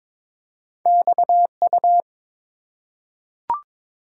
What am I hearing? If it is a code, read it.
Morse code, heard as a keyed tone, about 22 wpm, 700 Hz. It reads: XU
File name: practice / 22wpm / Two Letter-Number Combinations - 22wpm